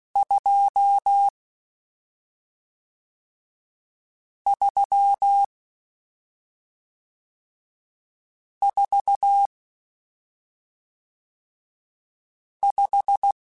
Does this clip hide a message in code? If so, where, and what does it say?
Yes: Morse code from 0.2 to 13.3 s, heard as a keyed tone, about 16 wpm, 800 Hz. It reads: 2 3 4 5